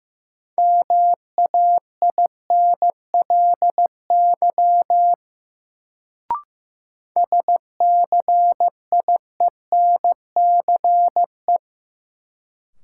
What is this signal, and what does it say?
Morse code, heard as a keyed tone, about 15 wpm, 700 Hz. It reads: MAINLY SCIENCE
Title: Sentences from Top 1400 Words - New Word Drill Alt - 15wpm